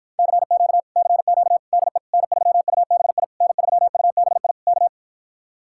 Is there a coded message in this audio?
Yes: Morse code from 0.2 to 4.9 s, heard as a keyed tone, about 53 wpm, 700 Hz. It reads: CQ CQ DE N3UBI N3UBI K